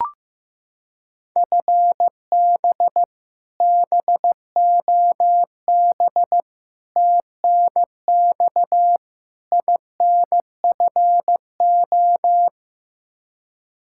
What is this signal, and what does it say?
Morse code, heard as a keyed tone, about 15 wpm, 700 Hz. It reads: FB BOB TNX INFO
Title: K1USN SST QSO Practice 15wpm